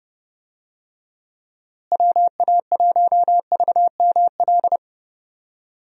Morse code, heard as a keyed tone, about 32 wpm, 700 Hz. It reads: WA1VML
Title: USA Call Signs - 6 Characters - Repeated 3x - 30wpm